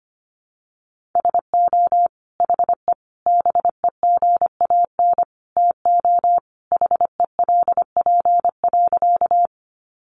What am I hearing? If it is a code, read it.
Morse code, heard as a keyed tone, about 25 wpm, 700 Hz. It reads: SO HE BEGAN TO HELP.
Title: Sentences from Top 300 Words 25wpm